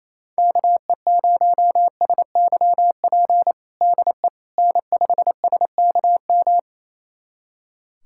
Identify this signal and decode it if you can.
Morse code, heard as a keyed tone, about 28 wpm, 700 Hz. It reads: KE0SYP DE N5SKM
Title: USA Callsign DE Callsign 28wpm